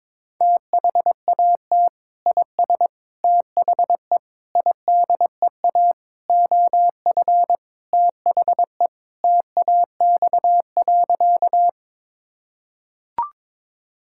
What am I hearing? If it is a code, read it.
Morse code, heard as a keyed tone, about 22 wpm, 700 Hz. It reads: THAT IS THE IDEA OF THE TAX.
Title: Sentences from Top 1000 Words - Review Part 5 - 22wpm